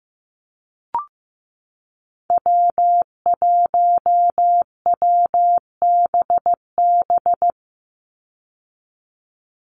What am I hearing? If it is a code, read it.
Morse code, heard as a keyed tone, about 15 wpm, 700 Hz. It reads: W1WBB
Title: Call Signs Repeated 2x - 15wpm